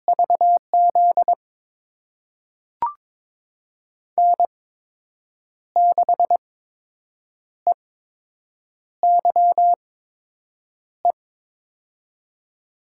Morse code, heard as a keyed tone, about 22 wpm, spaced as for 7 wpm, 700 Hz. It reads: VZ N6EYE
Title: USA Call Signs - 5 Characters - 8x Character Spacing - 22wpm